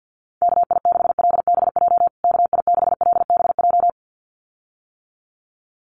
Morse code, heard as a keyed tone, about 50 wpm, 700 Hz. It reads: KI6LBJ KI6LBJ
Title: USA Call Signs Repeated 2x 50wpm